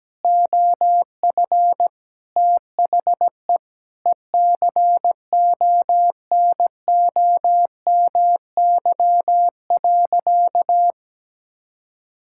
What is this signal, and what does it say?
Morse code, heard as a keyed tone, about 17 wpm, 700 Hz. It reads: OF THE ECONOMY.
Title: Sentences from Top 1000 Words - Review Part 6 - 17wpm